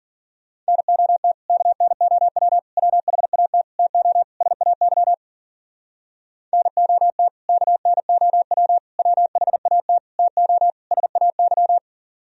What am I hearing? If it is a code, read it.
Morse code, heard as a keyed tone, about 52 wpm, 700 Hz. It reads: NOT KNOW WHAT TO SAY NOT KNOW WHAT TO SAY